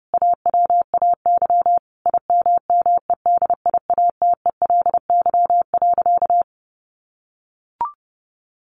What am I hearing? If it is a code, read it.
Morse code, heard as a keyed tone, about 30 wpm, 700 Hz. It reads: AWAY IMMEDIATELY.
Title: Sentences from Top 1100 Words 30wpm